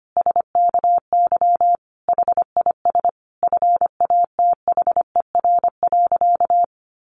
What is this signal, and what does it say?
Morse code, heard as a keyed tone, about 25 wpm, 700 Hz. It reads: SKY HIS FATHER.
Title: Sentences from Top 1200 Words 25wpm